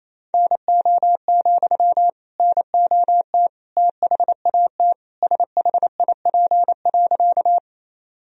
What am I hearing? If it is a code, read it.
Morse code, heard as a keyed tone, about 28 wpm, 700 Hz. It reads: NO, NOT THAT SHIP.
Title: Sentences from Top 1000 Words - Review Part 2 - 28wpm